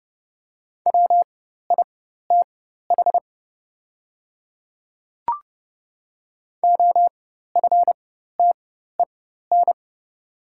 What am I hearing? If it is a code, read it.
Morse code, heard as a keyed tone, about 30 wpm, spaced as for 14 wpm, 690 Hz. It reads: WITH OFTEN
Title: 4x Character Spacing - Top 500 Words - 30wpm